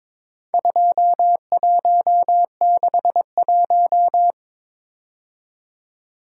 Morse code, heard as a keyed tone, about 23 wpm, 700 Hz. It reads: 2161
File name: practice / 22wpm / Four Letter-Number Combinations - 22wpm